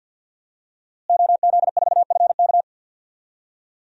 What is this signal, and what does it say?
Morse code, heard as a keyed tone, about 50 wpm, 700 Hz. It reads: OZ4FK
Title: DXCC-100 Call Signs - 5 Characters - Repeated 2x - 50wpm